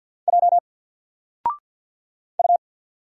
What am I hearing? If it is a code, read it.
Morse code, heard as a keyed tone, about 48 wpm, 700 Hz. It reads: J U
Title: Single Letter-Number Alt 50wpm